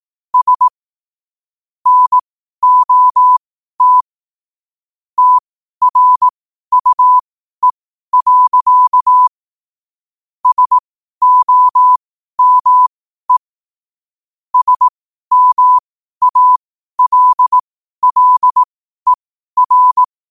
Morse code, heard as a keyed tone, about 18 wpm, spaced as for 12 wpm, 1000 Hz. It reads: S NOT TRUE. SOME SMALLER
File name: qotdfiles / 12wpm